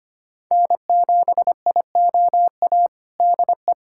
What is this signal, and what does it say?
Morse code, heard as a keyed tone, about 25 wpm, 700 Hz. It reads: N7IOA DE
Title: USA Callsign DE Callsign 25wpm